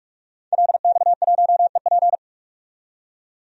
Morse code, heard as a keyed tone, about 45 wpm, 700 Hz. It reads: LX1EP
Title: DXCC-100 Call Signs - 5 Characters - 45wpm